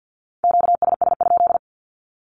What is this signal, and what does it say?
Morse code, heard as a keyed tone, about 50 wpm, 700 Hz. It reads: QSS?
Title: Q-Codes with Spoken Meaning - 50wpm